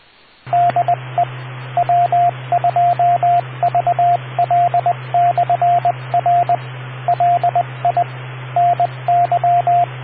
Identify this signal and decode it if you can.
Morse code, heard as a keyed tone, about 20 wpm, 700 Hz, with radio noise. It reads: DE W2VL/R LI NY